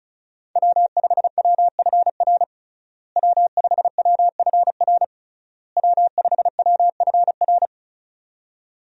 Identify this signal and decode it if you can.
Morse code, heard as a keyed tone, about 36 wpm, 700 Hz. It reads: W5WFR W5WFR W5WFR
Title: USA Call Signs - 5 Characters - Repeated 3x - 35wpm